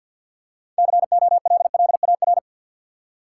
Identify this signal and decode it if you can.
Morse code, heard as a keyed tone, about 52 wpm, 700 Hz. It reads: COLLAR
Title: Six Letter Words - 50wpm